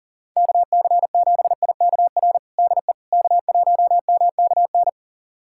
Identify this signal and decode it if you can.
Morse code, heard as a keyed tone, about 37 wpm, 700 Hz. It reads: KC7IKR DE K1MKN
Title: USA Callsign DE Callsign 40wpm